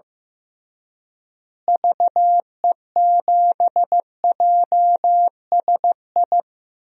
Morse code, heard as a keyed tone, about 15 wpm, 700 Hz. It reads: VE7JSI
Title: DXCC-100 Call Signs - 6 Characters - Repeated 3x - 15wpm